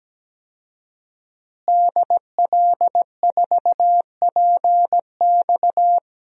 Morse code, heard as a keyed tone, about 17 wpm, 700 Hz. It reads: DL4PX
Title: DXCC-100 Call Signs - 5 Characters - Repeated 3x - 17wpm